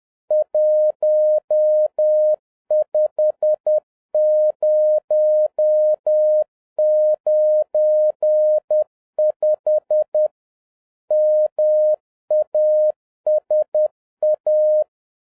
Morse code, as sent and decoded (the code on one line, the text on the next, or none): .---- ..... ----- ----. ..... / -- .- ... .-
15095 MASA